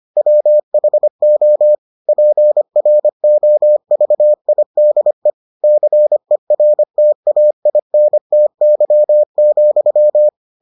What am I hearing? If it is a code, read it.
Morse code, heard as a keyed tone, about 25 wpm, 600 Hz. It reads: WHO PROVIDE CERTAINTY,